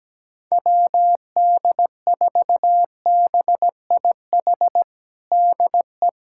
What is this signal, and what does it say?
Morse code, heard as a keyed tone, about 17 wpm, 700 Hz. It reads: WD4BIH DE
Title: USA Callsign DE Callsign 17wpm